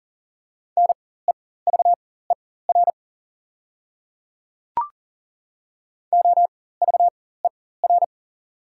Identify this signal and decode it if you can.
Morse code, heard as a keyed tone, about 40 wpm, spaced as for 19 wpm, 690 Hz. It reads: NEVER OVER
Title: 4x Character Spacing - Top 300 Words - 40wpm